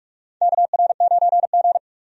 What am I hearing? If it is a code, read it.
Morse code, heard as a keyed tone, about 46 wpm, 700 Hz. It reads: KR9G
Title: USA Call Signs - 4 Characters - 45wpm